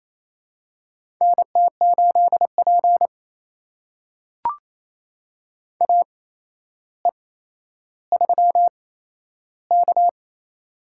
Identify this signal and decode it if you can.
Morse code, heard as a keyed tone, about 28 wpm, spaced as for 12 wpm, 700 Hz. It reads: NT8P AE3K
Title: USA Call Signs - 4 Characters - 8x Character Spacing - 28wpm